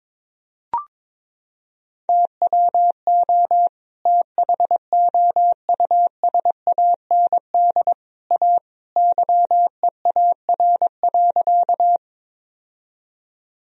Morse code, heard as a keyed tone, about 22 wpm, 700 Hz. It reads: TWO THOUSAND A YEAR.
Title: Sentences from Top 1000 Words - Review Part 3 - 22wpm